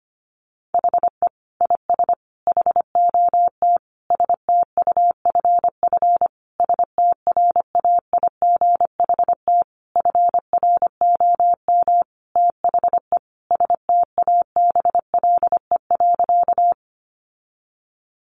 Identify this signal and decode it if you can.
Morse code, heard as a keyed tone, about 25 wpm, 700 Hz. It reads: HE IS HOT STUFF STRAIGHT FROM THE STABLE.
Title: Sentences from Top 1500 Words 25wpm